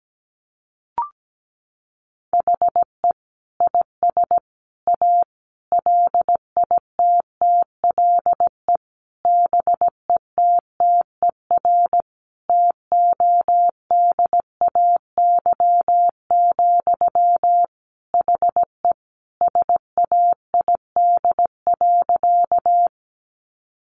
Morse code, heard as a keyed tone, about 17 wpm, 700 Hz. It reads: HE IS A LITTLE BETTER TODAY, HE SAID.
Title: Sentences from Top 600 Words 17wpm